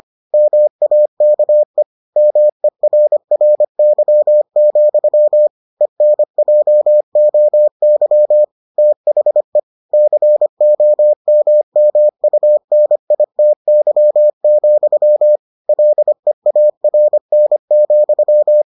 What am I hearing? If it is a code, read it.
Morse code, heard as a keyed tone, about 25 wpm, 600 Hz. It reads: MAKE MERRY, ENJOY THE COMMUNITY, LEARN,